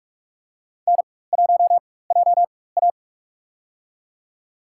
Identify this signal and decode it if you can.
Morse code, heard as a keyed tone, about 44 wpm, spaced as for 21 wpm, 700 Hz. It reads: N1JA
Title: USA Call Signs - 4 Characters - 4x Character Spacing - 45wpm